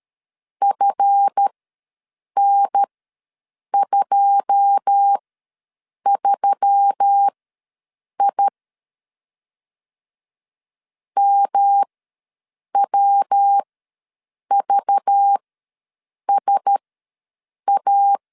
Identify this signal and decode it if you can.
Morse code, heard as a keyed tone, about 13 wpm, spaced as for 7 wpm, 800 Hz. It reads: FN23I MWVSA
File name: LEZIONI / 9 2 ws 12wpm 3406